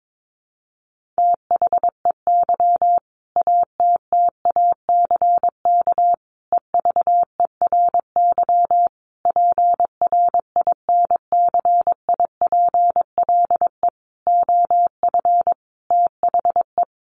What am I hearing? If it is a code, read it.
Morse code, heard as a keyed tone, about 22 wpm, 700 Hz. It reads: THEY ATTACK EVERY PRINCIPLE OF THE
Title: Sentences from Top 1500 Words 22wpm